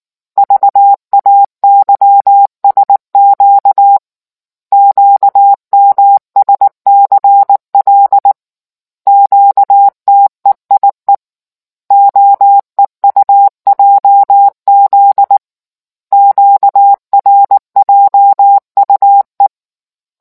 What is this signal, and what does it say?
Morse code, heard as a keyed tone, about 19 wpm, 800 Hz. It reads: VAYSQ QMSCL QTEIE OEUJZ QRJUE